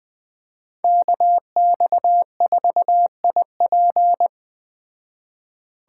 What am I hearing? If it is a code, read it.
Morse code, heard as a keyed tone, about 19 wpm, 700 Hz. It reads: KX4IP